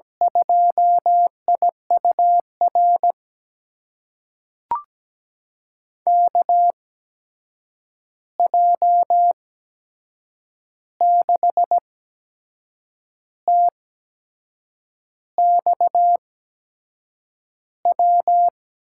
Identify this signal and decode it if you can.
Morse code, heard as a keyed tone, about 17 wpm, spaced as for 6 wpm, 700 Hz. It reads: 2IUR KJ6TXW
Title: USA Call Signs - 8x Character Spacing - 17wpm